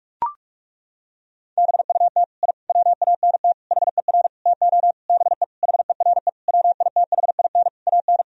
Morse code, heard as a keyed tone, about 42 wpm, 700 Hz. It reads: BUT I WANT HER TO BE HERE WITHIN AN
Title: Sentences from Top 1000 Words - Review Part 4 - 45wpm